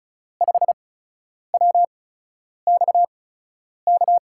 Morse code, heard as a keyed tone, about 36 wpm, spaced as for 10 wpm, 700 Hz. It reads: 5WXK
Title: USA Call Signs - 6 Characters - 8x Character Spacing - 35wpm